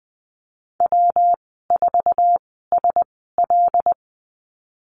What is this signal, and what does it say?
Morse code, heard as a keyed tone, about 20 wpm, spaced as for 15 wpm, 700 Hz. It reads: W4SL
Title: 147 Lesson 24g - Letter y - Call Signs 20wpm